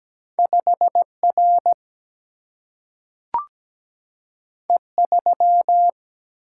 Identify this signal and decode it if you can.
Morse code, heard as a keyed tone, about 17 wpm, 700 Hz. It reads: 5R E3
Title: Single-Letter Single-Number 17wpm